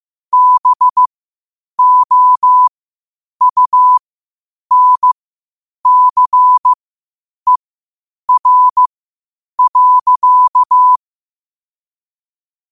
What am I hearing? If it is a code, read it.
Morse code, heard as a keyed tone, about 15 wpm, spaced as for 8 wpm, 1000 Hz. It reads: BOUNCER.
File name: qotdfiles / 8wpm